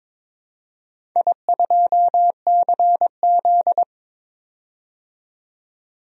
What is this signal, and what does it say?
Morse code, heard as a keyed tone, about 23 wpm, 700 Hz. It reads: I2CZ
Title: DXCC-100 Call Signs - 4 Characters - 22wpm